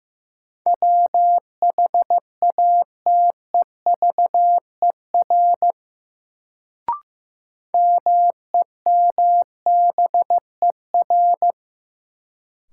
Morse code, heard as a keyed tone, about 15 wpm, 700 Hz. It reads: WHATEVER MEMBER
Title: Sentences from Top 1100 Words - New Word Drill Alt - 15wpm